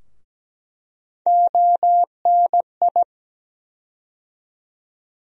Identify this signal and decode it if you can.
Morse code, heard as a keyed tone, about 17 wpm, 700 Hz. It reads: ONI
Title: USA Call Sign Postfixes - 3 Letters 17wpm